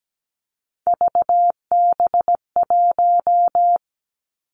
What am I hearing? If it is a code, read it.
Morse code, heard as a keyed tone, about 17 wpm, 700 Hz. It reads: VB1
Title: Three Challenging Letter-Number Combinations 17wpm